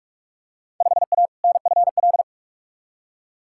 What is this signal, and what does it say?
Morse code, heard as a keyed tone, about 46 wpm, 700 Hz. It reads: 5A NFL
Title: Field Day Exchanges - 45wpm